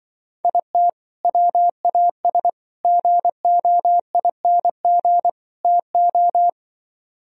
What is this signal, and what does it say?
Morse code, heard as a keyed tone, about 24 wpm, 700 Hz. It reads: IT WAS GOING TO
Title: Speed-Racing - Sets of 4 Words 20wpm